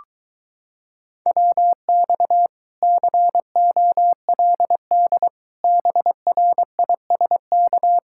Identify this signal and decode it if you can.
Morse code, heard as a keyed tone, about 23 wpm, 700 Hz. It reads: WX COLD BRISK